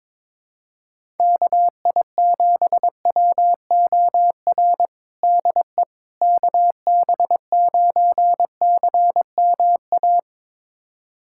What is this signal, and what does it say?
Morse code, heard as a keyed tone, about 22 wpm, 700 Hz. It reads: KI7WOR DE KB9CMA
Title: USA Callsign DE Callsign - 6 Characters - 22wpm